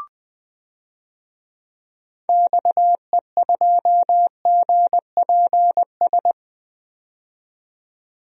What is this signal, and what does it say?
Morse code, heard as a keyed tone, about 20 wpm, 700 Hz. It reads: XE2GPS
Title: DXCC-100 Call Signs - 6 Characters - 20wpm